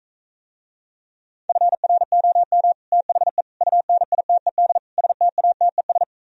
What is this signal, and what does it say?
Morse code, heard as a keyed tone, about 45 wpm, 700 Hz. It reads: FROM THE UNITED STATES